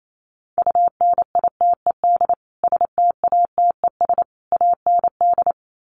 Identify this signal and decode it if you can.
Morse code, heard as a keyed tone, about 28 wpm, 700 Hz. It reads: UNITED STATES AND
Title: Sets of 3 Words 28wpm